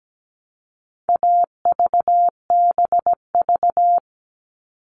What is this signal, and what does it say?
Morse code, heard as a keyed tone, about 17 wpm, 700 Hz. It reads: AVBV
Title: All Cut Numbers - 4-Digit 17wpm